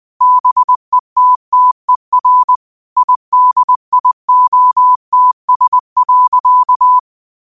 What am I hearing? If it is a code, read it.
Morse code, heard as a keyed tone, about 20 wpm, 1000 Hz. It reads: BETTER IDIOTS.
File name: qotdfiles / 20wpm